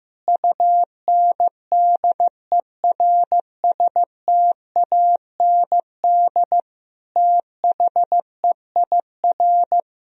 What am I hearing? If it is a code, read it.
Morse code, heard as a keyed tone, about 15 wpm, 700 Hz. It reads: UNDERSTAND THEIR
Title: Sentences from Top 1000 Words - Review Part 3 - 15wpm